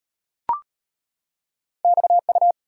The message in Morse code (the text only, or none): XU